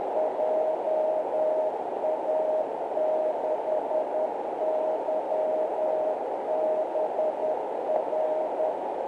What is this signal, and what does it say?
Morse code, heard as a keyed tone, about 10 wpm, 620 Hz, with noise. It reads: JA6YBR